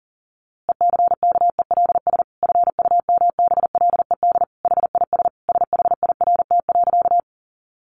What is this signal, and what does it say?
Morse code, heard as a keyed tone, about 40 wpm, 700 Hz. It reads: ECKELS FUMBLED HIS SHIRT.
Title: A Sound Of Thunder 40wpm